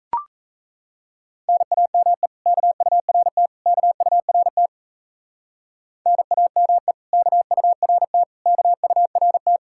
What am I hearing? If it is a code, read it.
Morse code, heard as a keyed tone, about 33 wpm, 700 Hz. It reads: NAME KURT KURT NAME KURT KURT
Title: Speed-Racing Common QSO Phrases - 28wpm